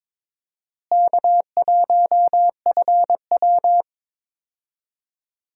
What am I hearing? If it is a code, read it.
Morse code, heard as a keyed tone, about 22 wpm, 700 Hz. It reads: K1FW
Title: USA Call Signs - 4 Characters - Repeated 3x - 22wpm